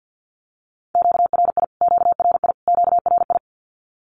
Morse code, heard as a keyed tone, about 48 wpm, 700 Hz. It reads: QRI QRI QRI
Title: Q-Codes Repeated 3x 50wpm